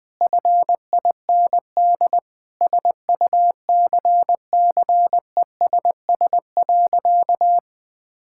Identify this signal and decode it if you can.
Morse code, heard as a keyed tone, about 20 wpm, 700 Hz. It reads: FIND SUCCESS.
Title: Sentences from Top 1000 Words - Review Part 6 - 20wpm